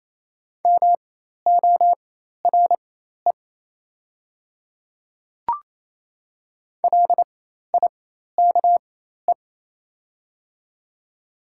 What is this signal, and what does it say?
Morse code, heard as a keyed tone, about 28 wpm, spaced as for 13 wpm, 700 Hz. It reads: MORE LIKE
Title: 4x Character Spacing - Top 100 Words - 28wpm